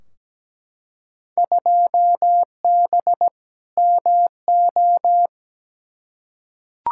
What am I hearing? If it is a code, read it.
Morse code, heard as a keyed tone, about 17 wpm, 700 Hz. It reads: 2B MO
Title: Field Day Exchanges Encore - 17wpm